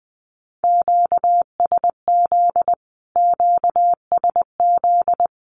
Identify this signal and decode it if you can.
Morse code, heard as a keyed tone, about 20 wpm, 700 Hz. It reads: QSZ QSZ
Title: Q-Codes Repeated 3x 20wpm